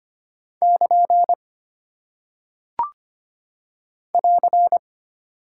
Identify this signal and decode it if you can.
Morse code, heard as a keyed tone, about 25 wpm, 700 Hz. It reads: <KN> <AR>
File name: Single Punctuation-Prosign - 25wpm